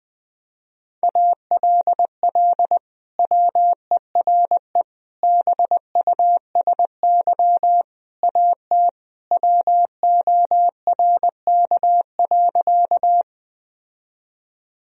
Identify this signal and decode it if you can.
Morse code, heard as a keyed tone, about 21 wpm, 700 Hz. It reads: ALL WERE BUSY AT WORK.
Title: Sentences from Top 1000 Words - Review Part 3 - 20wpm